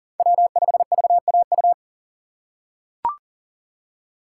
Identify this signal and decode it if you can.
Morse code, heard as a keyed tone, about 40 wpm, 700 Hz. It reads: W5VAU
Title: USA Call Signs - 5 Characters - 6x Character Spacing - 40wpm